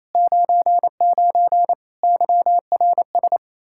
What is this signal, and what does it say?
Morse code, heard as a keyed tone, about 27 wpm, 700 Hz. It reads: 99 YRS
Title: Practice QSO - Exchange 3 - More Info - 28wpm